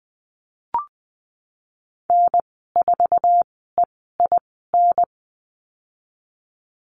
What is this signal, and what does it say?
Morse code, heard as a keyed tone, about 20 wpm, spaced as for 15 wpm, 700 Hz. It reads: N4EIN
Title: 082 Lesson 14g - Number 2 - Call Signs 20wpm